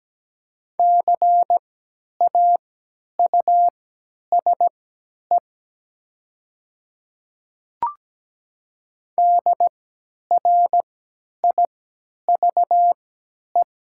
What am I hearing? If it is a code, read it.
Morse code, heard as a keyed tone, about 17 wpm, spaced as for 10 wpm, 690 Hz. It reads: CAUSE DRIVE
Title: 3x Character Spacing - Top 500 Words - 17wpm